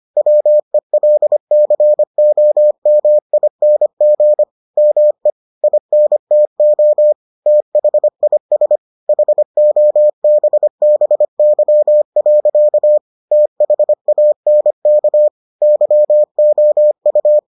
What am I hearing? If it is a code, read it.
Morse code, heard as a keyed tone, about 25 wpm, 600 Hz. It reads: WELCOMING ME INTO THIS HOBBY. THANK YOU